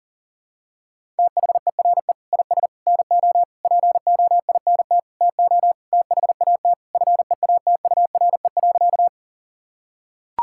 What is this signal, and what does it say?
Morse code, heard as a keyed tone, about 40 wpm, 700 Hz. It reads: THERE IS NO POINT TO THAT FEATURE.
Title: Sentences from Top 1000 Words - Review Part 5 - 40wpm